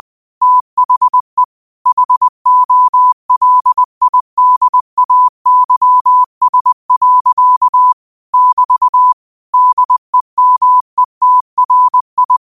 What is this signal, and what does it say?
Morse code, heard as a keyed tone, about 20 wpm, 1000 Hz. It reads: THE HOLIDAYS. <BT> DEMETRI
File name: qotdfiles / 20wpm